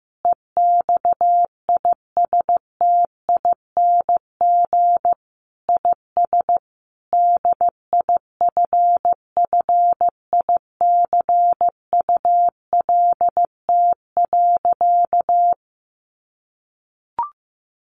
Morse code, heard as a keyed tone, about 14 wpm, 700 Hz. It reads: EXISTING IS DIFFICULT.
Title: Sentences from Top 1500 Words 15wpm